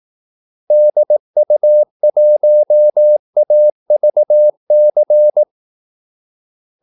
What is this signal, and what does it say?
Morse code, heard as a keyed tone, about 18 wpm, 600 Hz. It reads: DU1AVC
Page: Kaldesignaler 18 wpm